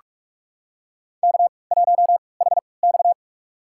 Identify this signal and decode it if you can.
Morse code, heard as a keyed tone, about 46 wpm, spaced as for 26 wpm, 700 Hz. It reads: K1HX
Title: USA Call Signs - 4 Characters - 3x Character Spacing - 45wpm